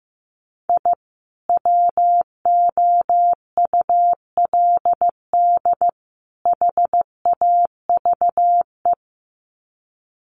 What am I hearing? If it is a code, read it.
Morse code, heard as a keyed tone, about 15 wpm, 700 Hz. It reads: I WOULD HAVE
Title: Sets of 3 Words 15wpm